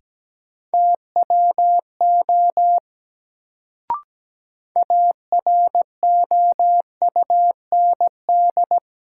Morse code, heard as a keyed tone, about 17 wpm, 700 Hz. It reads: TWO AROUND
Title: Sets of 2 Words - New Word Drill - 17wpm